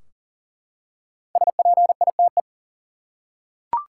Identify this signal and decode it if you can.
Morse code, heard as a keyed tone, about 40 wpm, 700 Hz. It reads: SPITE
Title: Five Letter Words - 40wpm